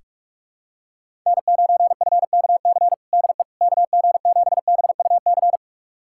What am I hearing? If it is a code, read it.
Morse code, heard as a keyed tone, about 45 wpm, 700 Hz. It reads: N9FKC DE KG7BUC
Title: USA Callsign DE Callsign 45wpm